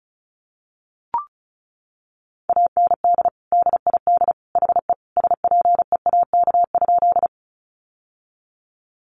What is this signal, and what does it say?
Morse code, heard as a keyed tone, about 34 wpm, 700 Hz. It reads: AND DID HE SPEAK?
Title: Sentences from Top 1100 Words 35wpm